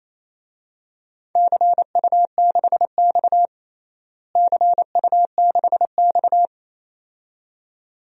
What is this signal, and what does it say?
Morse code, heard as a keyed tone, about 28 wpm, 700 Hz. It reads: CU6X CU6X
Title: DXCC-100 Call Signs - 4 Characters - Repeated 2x - 28wpm